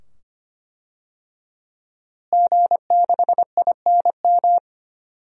Morse code, heard as a keyed tone, about 25 wpm, 700 Hz. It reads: G6INM
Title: DXCC-100 Call Signs - 5 Characters - Repeated 2x - 25wpm